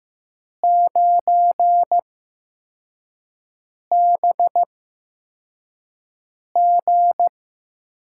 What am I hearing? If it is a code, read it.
Morse code, heard as a keyed tone, about 15 wpm, spaced as for 4 wpm, 700 Hz. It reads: 9BG
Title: USA Call Signs - 4 Characters - 8x Character Spacing - 15wpm